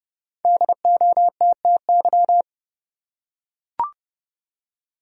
Morse code, heard as a keyed tone, about 30 wpm, 700 Hz. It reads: DOTTY
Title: USA Common Names - Max 5 Letters - 30wpm